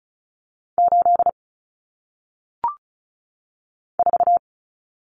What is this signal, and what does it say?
Morse code, heard as a keyed tone, about 36 wpm, 700 Hz. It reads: 8 4
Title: Numbers - 1-Digit 35wpm